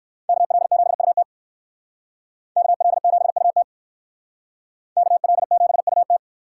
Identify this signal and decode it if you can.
Morse code, heard as a keyed tone, about 57 wpm, 700 Hz. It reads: KL7UT KL7UT KL7UT